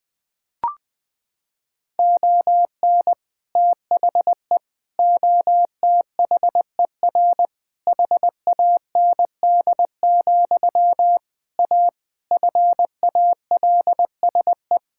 Morse code, heard as a keyed tone, about 20 wpm, 700 Hz. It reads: ON THE OTHER HAND, A FALSE
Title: Sentences from Top 1600 Words 20wpm